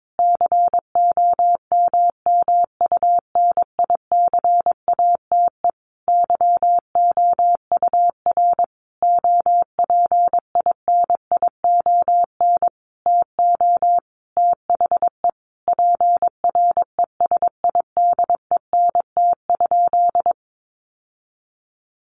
Morse code, heard as a keyed tone, about 22 wpm, 700 Hz. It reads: COMMUNICATE YOUR OPINION TO THE PRESIDENT?
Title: Sentences from Top 1300 Words 22wpm